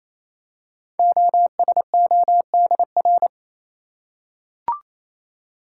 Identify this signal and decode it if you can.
Morse code, heard as a keyed tone, about 27 wpm, 700 Hz. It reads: OSODR
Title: Code Groups - 5 Letters - 28wpm